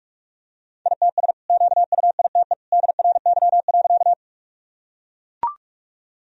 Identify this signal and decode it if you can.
Morse code, heard as a keyed tone, about 41 wpm, 700 Hz. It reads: ITS QUITE DRY.
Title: Sentences from Top 1000 Words - Review Part 3 - 45wpm